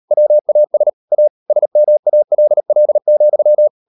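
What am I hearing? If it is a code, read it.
Morse code, heard as a keyed tone, about 38 wpm, 600 Hz. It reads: WAS A SMALL,
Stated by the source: Morse code practice at 38 WPM, sending Ken Follett's novel Never in English, from CW med Gnister.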